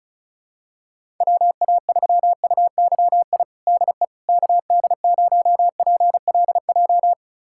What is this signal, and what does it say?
Morse code, heard as a keyed tone, about 34 wpm, 700 Hz. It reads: WA3UYI DE KD0PLJ
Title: USA Callsign DE Callsign - 6 Characters - 35wpm